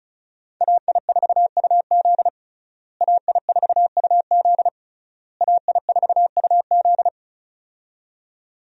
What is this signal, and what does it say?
Morse code, heard as a keyed tone, about 36 wpm, 700 Hz. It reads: AI4UZ AI4UZ AI4UZ
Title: USA Call Signs - 5 Characters - Repeated 3x - 35wpm